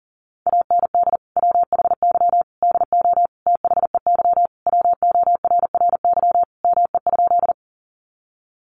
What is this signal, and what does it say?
Morse code, heard as a keyed tone, about 40 wpm, 700 Hz. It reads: AND WHY DO THEY WORRY ME?
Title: Sentences from Top 1100 Words 40wpm